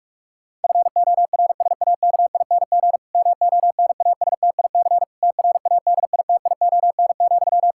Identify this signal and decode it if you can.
Morse code, heard as a keyed tone, about 46 wpm, 700 Hz. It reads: FORSAKING MONASTIC TRADITION,
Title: Pangram Challenge v2 - 45wpm